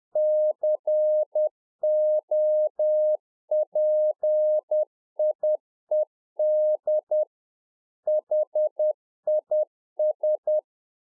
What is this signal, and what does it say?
Morse code, heard as a keyed tone, about 10 wpm, 610 Hz. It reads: COPIED HIS
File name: UNFOGETTABLE QSO 10WPM